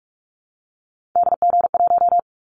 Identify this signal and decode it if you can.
Morse code, heard as a keyed tone, about 46 wpm, 700 Hz. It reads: DG1
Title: Callsign Prefixes for Top 50 DXCC - 45wpm